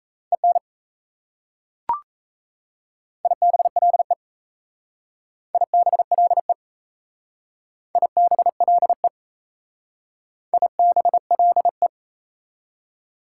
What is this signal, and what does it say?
Morse code, heard as a keyed tone, about 34 wpm, 690 Hz. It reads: EN IBLE IBLE IBLE IBLE
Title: Speed-Racing Prefixes and Suffixes 28wpm